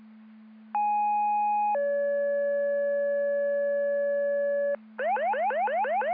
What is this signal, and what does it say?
Two-tone page, heard as a one-second tone, then a three-second tone, about 830 then 570 Hz, with noise. It is heard from 0.7 to 4.8 s.